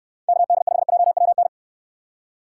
Morse code, heard as a keyed tone, about 69 wpm, 700 Hz. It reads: KD5JWN